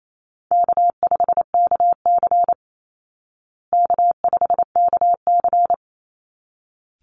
Morse code, heard as a keyed tone, about 28 wpm, 700 Hz. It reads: K5KC K5KC
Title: Easy Call Signs Repeated 2x - 28wpm